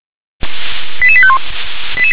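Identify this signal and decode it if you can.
Five-tone selective call (ZVEI), heard as quick runs of five tones, tones of about 70 ms. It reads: 88851
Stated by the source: ZVEI five-tone selective call 88911 received by SDR.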